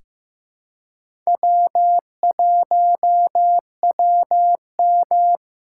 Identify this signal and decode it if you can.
Morse code, heard as a keyed tone, about 15 wpm, 700 Hz. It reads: W1WM
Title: USA Call Signs - 4 Characters - 4x Character Spacing - 15wpm